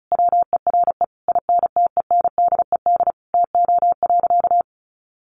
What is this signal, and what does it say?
Morse code, heard as a keyed tone, about 34 wpm, 700 Hz. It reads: WERE INTENDED TO.
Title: A Sound Of Thunder 35wpm